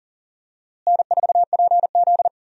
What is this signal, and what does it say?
Morse code, heard as a keyed tone, about 40 wpm, 700 Hz. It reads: N4PZ
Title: USA Call Signs - 4 Characters - Repeated 2x - 40wpm